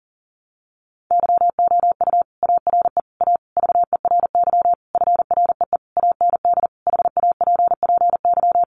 Morse code, heard as a keyed tone, about 40 wpm, 700 Hz. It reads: YOU ARE A VERY FREE AND HAPPY
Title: Sentences from Top 1000 Words 40wpm